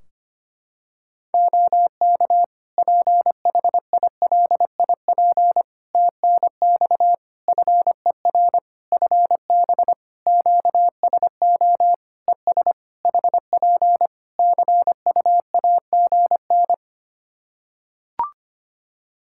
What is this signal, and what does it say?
Morse code, heard as a keyed tone, about 25 wpm, 700 Hz. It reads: OK PHILIP TNX FER FB QSO ES HP CUAGN
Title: K1USN SST QSO Practice 25wpm